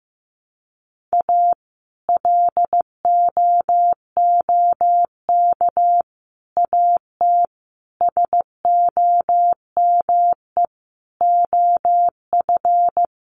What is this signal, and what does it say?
Morse code, heard as a keyed tone, about 15 wpm, 700 Hz. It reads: A LOOK AT SOME OF
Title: Sets of 5 Words Encore2 15wpm